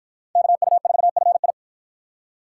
Morse code, heard as a keyed tone, about 53 wpm, 700 Hz. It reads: KU4FI